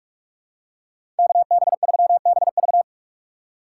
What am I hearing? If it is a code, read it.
Morse code, heard as a keyed tone, about 46 wpm, 700 Hz. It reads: KB3BV
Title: USA Call Signs - 5 Characters - 2x Character Spacing - 45wpm